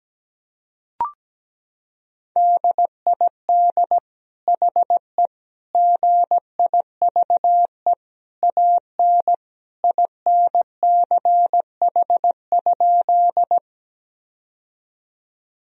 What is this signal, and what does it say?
Morse code, heard as a keyed tone, about 17 wpm, 700 Hz. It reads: DID HE GIVE AN INCH?
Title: Sentences from Top 1000 Words - Review Part 3 - 17wpm